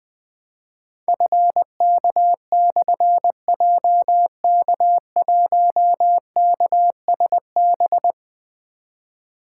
Morse code, heard as a keyed tone, about 20 wpm, 700 Hz. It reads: FK/JK1KSB
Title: Difficult Call Signs 20wpm